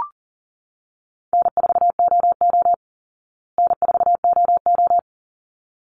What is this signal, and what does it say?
Morse code, heard as a keyed tone, about 40 wpm, 700 Hz. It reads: N4OO N4OO
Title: Call Signs Repeated 3x - 40wpm